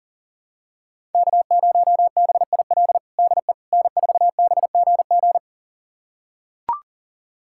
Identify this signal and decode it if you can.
Morse code, heard as a keyed tone, about 37 wpm, 700 Hz. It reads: K0BIL DE N4BGG
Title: USA Callsign DE Callsign - 5 Characters - 40wpm